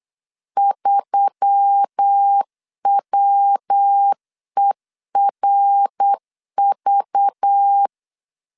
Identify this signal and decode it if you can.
Morse code, heard as a keyed tone, about 8 wpm, 800 Hz. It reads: 3WERV
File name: LEZIONI / 6 PKX0 / 6 PKX0 8wpm 3341